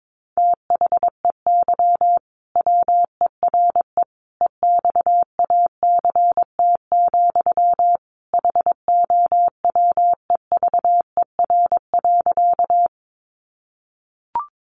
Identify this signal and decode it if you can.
Morse code, heard as a keyed tone, about 22 wpm, 700 Hz. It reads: THEY WERE EXACT, HOWEVER.
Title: Sentences from Top 1300 Words 22wpm